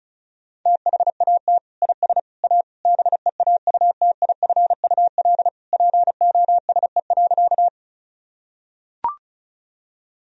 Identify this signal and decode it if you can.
Morse code, heard as a keyed tone, about 36 wpm, 700 Hz. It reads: THAT IS A BEAUTIFUL POSE.
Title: Sentences from Top 1000 Words - Review Part 2 - 35wpm